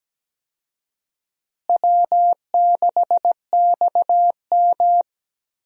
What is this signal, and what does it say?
Morse code, heard as a keyed tone, about 17 wpm, 700 Hz. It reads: W6XM
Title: USA Call Signs - 4 Characters - Repeated 2x - 17wpm